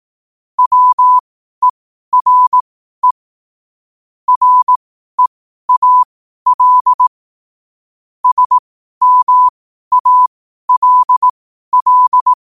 Morse code, heard as a keyed tone, about 18 wpm, spaced as for 12 wpm, 1000 Hz. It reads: WERE REAL SMALL
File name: qotdfiles / 12wpm